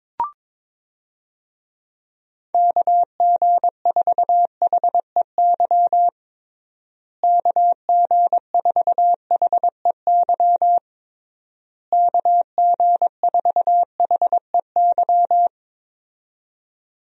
Morse code, heard as a keyed tone, about 22 wpm, 700 Hz. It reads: KG4HEY KG4HEY KG4HEY
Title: USA Call Signs - 6 Characters - Repeated 3x - 22wpm